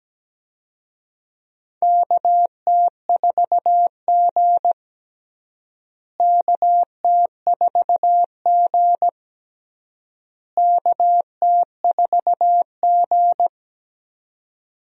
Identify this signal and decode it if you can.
Morse code, heard as a keyed tone, about 17 wpm, 700 Hz. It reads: KT4G KT4G KT4G
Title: USA Call Signs - 4 Characters - Repeated 3x - 17wpm